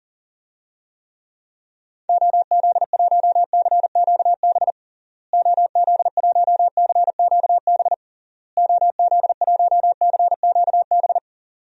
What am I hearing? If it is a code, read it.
Morse code, heard as a keyed tone, about 40 wpm, 700 Hz. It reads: OZ1CQB OZ1CQB OZ1CQB
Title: DXCC-100 Call Signs - 6 Characters - Repeated 3x - 40wpm